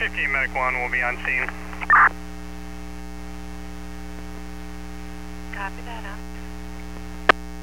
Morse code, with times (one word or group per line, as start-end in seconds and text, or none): none